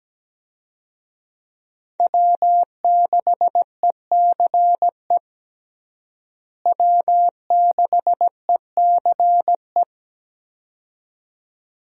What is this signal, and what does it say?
Morse code, heard as a keyed tone, about 17 wpm, 700 Hz. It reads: W6ECE W6ECE
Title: USA Call Signs - 5 Characters - Repeated 2x - 17wpm